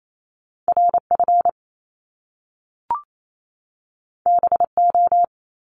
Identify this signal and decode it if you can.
Morse code, heard as a keyed tone, about 28 wpm, 700 Hz. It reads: RF BO
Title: Top Two Letter Combinations in Words 28wpm